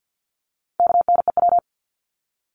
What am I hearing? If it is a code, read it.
Morse code, heard as a keyed tone, about 50 wpm, 700 Hz. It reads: KNEW
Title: Top 500 Words 50wpm